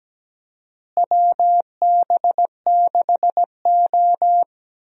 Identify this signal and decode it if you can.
Morse code, heard as a keyed tone, about 17 wpm, 700 Hz. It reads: WB6O
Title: USA Call Signs - 4 Characters - Repeated 2x - 17wpm